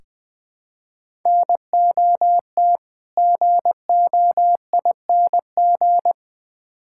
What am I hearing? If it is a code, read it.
Morse code, heard as a keyed tone, about 20 wpm, 700 Hz. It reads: NOT GOING
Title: Build A Phrase - 5-Word Phrases - 20wpm